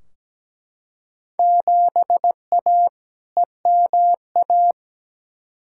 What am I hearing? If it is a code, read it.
Morse code, heard as a keyed tone, about 17 wpm, 700 Hz. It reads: 7A EMA
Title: Field Day Exchanges Encore - 17wpm